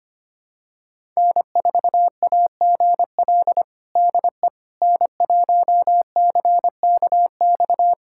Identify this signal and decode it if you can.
Morse code, heard as a keyed tone, about 25 wpm, 700 Hz. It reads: N4AGL DE N1CKX
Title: USA Callsign DE Callsign - 5 Characters - 25wpm